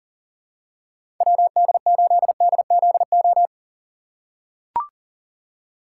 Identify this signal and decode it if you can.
Morse code, heard as a keyed tone, about 40 wpm, 700 Hz. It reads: WD8DZO
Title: USA Call Signs - 4x Character Spacing - 40wpm